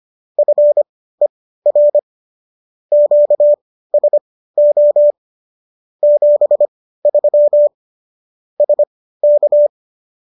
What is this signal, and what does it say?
Morse code, heard as a keyed tone, about 25 wpm, spaced as for 15 wpm, 590 Hz. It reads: FER QSO 73 SK